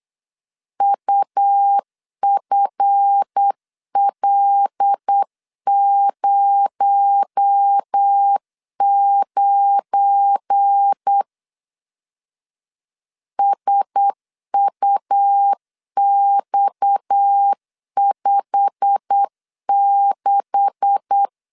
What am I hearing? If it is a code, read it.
Morse code, heard as a keyed tone, about 8 wpm, 800 Hz. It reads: UFL09 SUX56
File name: LEZIONI / 10 SIGNS / 10A SLASH 8wpm 3452